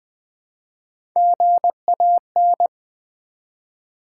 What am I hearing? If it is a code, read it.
Morse code, heard as a keyed tone, about 20 wpm, 700 Hz. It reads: GAN
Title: Build A Word - 1 Letter At A Time - 4-Letter Words - 20wpm